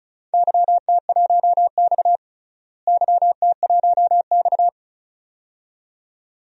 Morse code, heard as a keyed tone, about 36 wpm, 700 Hz. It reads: YT1X YT1X
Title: DXCC-100 Call Signs - 4 Characters - Repeated 2x - 35wpm